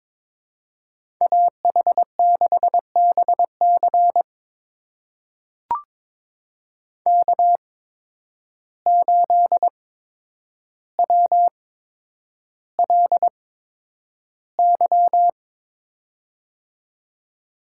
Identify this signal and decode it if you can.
Morse code, heard as a keyed tone, about 22 wpm, spaced as for 9 wpm, 700 Hz. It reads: AH6BC K8WLY
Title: USA Call Signs - 5 Characters - 8x Character Spacing - 22wpm